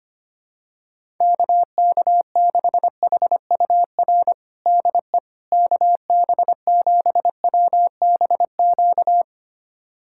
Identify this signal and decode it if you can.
Morse code, heard as a keyed tone, about 25 wpm, 700 Hz. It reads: KK6HUR DE KB7WBQ
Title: USA Callsign DE Callsign - 6 Characters - 25wpm